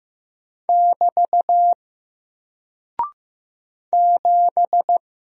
Morse code, heard as a keyed tone, about 15 wpm, 700 Hz. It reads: <BT> 7
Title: Single Character (Letter:Number:Punctuation:Prosign) Alt 15wpm